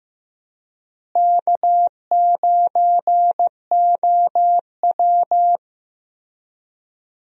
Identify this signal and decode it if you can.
Morse code, heard as a keyed tone, about 15 wpm, 700 Hz. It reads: K9OW
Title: USA Call Signs - 4 Characters - Repeated 3x - 15wpm